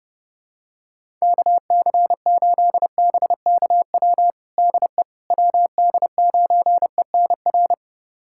Morse code, heard as a keyed tone, about 30 wpm, 700 Hz. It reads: KC8BKW DE WD9ENR
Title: USA Callsign DE Callsign - 6 Characters - 30wpm